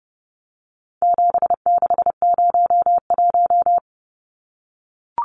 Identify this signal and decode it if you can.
Morse code, heard as a keyed tone, about 30 wpm, 700 Hz. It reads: 7601
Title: Numbers - 4-Digit 30wpm